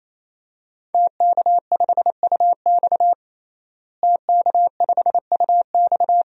Morse code, heard as a keyed tone, about 28 wpm, 700 Hz. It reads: TK5UX TK5UX
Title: DXCC-100 Call Signs - 5 Characters - Repeated 3x - 28wpm